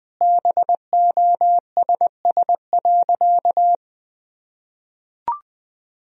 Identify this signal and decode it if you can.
Morse code, heard as a keyed tone, about 20 wpm, 700 Hz. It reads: BOSS.
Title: Sentences from Top 1000 Words - Review Part 4 - 20wpm